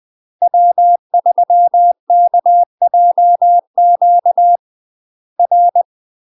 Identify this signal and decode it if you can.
Morse code, heard as a keyed tone, about 20 wpm, 700 Hz. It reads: W3KJQ R
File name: UK MASTER QSOs TRAINING / 20WPM / UK MASTER QSOs 0050